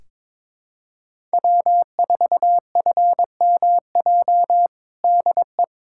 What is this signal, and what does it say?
Morse code, heard as a keyed tone, about 22 wpm, 700 Hz. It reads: W4FMJ DE
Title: USA Callsign DE Callsign - 5 Characters - 22wpm